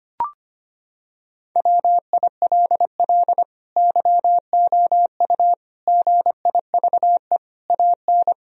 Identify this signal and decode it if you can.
Morse code, heard as a keyed tone, about 25 wpm, 700 Hz. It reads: WILL YOU GIVE AN
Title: Sentences from Top 1000 Words - Review Part 3 - 25wpm